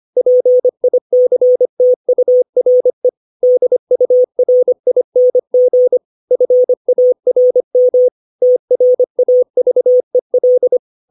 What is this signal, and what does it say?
Morse code, heard as a keyed tone, about 25 wpm, 500 Hz. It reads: PICTURE DURING FARM TRAVEL
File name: 632 words / 632 Words 25WPM